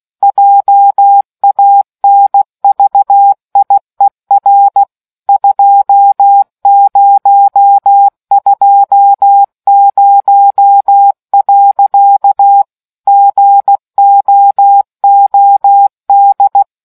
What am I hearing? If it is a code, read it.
Morse code, heard as a keyed tone, about 16 wpm, 800 Hz. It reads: JANVIER 2020. GOOD